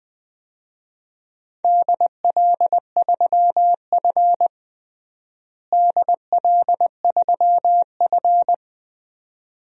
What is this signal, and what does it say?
Morse code, heard as a keyed tone, about 21 wpm, 700 Hz. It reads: DL3F DL3F
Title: DXCC-100 Call Signs - 4 Characters - Repeated 2x - 20wpm